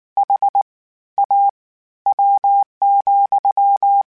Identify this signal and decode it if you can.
Morse code, heard as a keyed tone, about 19 wpm, spaced as for 13 wpm, 800 Hz. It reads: HAW,
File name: Misc / Misc-mp3s / WordPairs sp @18